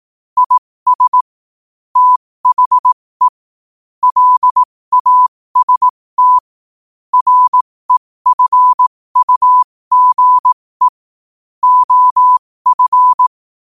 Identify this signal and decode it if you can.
Morse code, heard as a keyed tone, about 18 wpm, spaced as for 15 wpm, 1000 Hz. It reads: IS THE LAST REFUGE OF
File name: qotdfiles / 15wpm